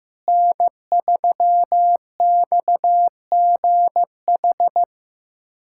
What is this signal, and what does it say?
Morse code, heard as a keyed tone, about 15 wpm, 700 Hz. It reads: N3XGH
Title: USA Callsign DE Callsign 15wpm